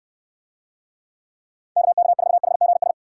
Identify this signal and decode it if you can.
Morse code, heard as a keyed tone, about 62 wpm, 700 Hz. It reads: KK4HGS